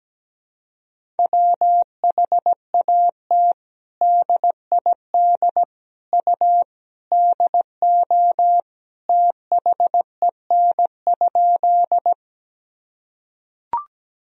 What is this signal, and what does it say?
Morse code, heard as a keyed tone, about 17 wpm, 700 Hz. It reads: WHAT DID U DO THEN?
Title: Sentences from Top 100 Words and Abbreviations 17wpm